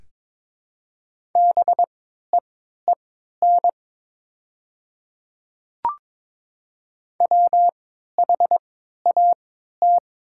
Morse code, heard as a keyed tone, about 22 wpm, spaced as for 12 wpm, 700 Hz. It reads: BEEN WHAT
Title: 3x Character Spacing - Top 200 Words - 22wpm